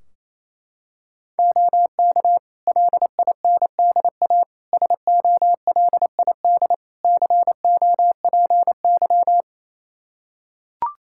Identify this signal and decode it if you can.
Morse code, heard as a keyed tone, about 28 wpm, 700 Hz. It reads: OK LINDA SOLID COPY
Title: Practice QSO - Exchange 3 - More Info - 28wpm